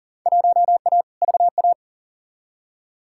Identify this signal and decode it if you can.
Morse code, heard as a keyed tone, about 40 wpm, 700 Hz. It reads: 1A VA
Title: Field Day Exchanges Encore - 40wpm